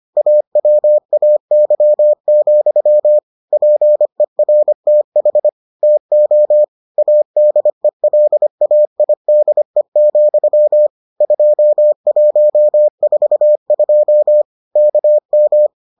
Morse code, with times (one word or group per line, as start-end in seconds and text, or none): 0.2-3.2 AWAY,
3.5-5.5 PERTH
5.8-6.6 TO
7.0-10.9 ADELAIDE,
11.2-14.4 2142
14.8-15.7 KM